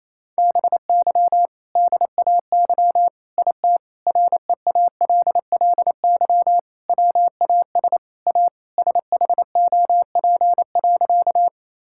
Morse code, heard as a keyed tone, about 27 wpm, 700 Hz. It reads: BY DAY IT REALLY WAS A SHOP.
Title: Sentences from Top 1600 Words 28wpm